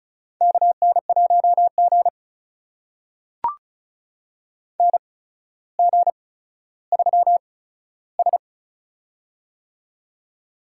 Morse code, heard as a keyed tone, about 36 wpm, spaced as for 15 wpm, 700 Hz. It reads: KN1G NG3S
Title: USA Call Signs - 4 Characters - 8x Character Spacing - 35wpm